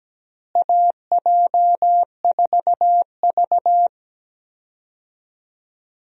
Morse code, heard as a keyed tone, about 17 wpm, 700 Hz. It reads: AJ4V
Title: USA Call Signs - 4 Characters - Repeated 3x - 17wpm